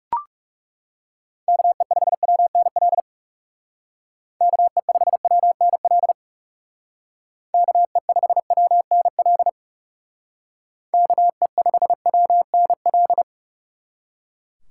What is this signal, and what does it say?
Morse code, heard as a keyed tone, about 37 wpm, 700 Hz. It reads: KE5WNL KE5WNL KE5WNL KE5WNL
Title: Speed Racing - USA Call Signs 30wpm